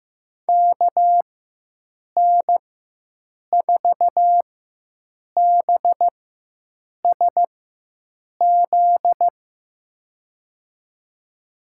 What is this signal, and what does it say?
Morse code, heard as a keyed tone, about 15 wpm, spaced as for 7 wpm, 700 Hz. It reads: KN4BSZ
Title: USA Call Signs - 6 Characters - 4x Character Spacing - 15wpm